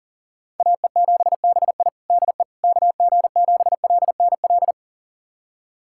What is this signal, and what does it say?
Morse code, heard as a keyed tone, about 40 wpm, 700 Hz. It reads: AE7BI DE KG7LNL
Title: USA Callsign DE Callsign 40wpm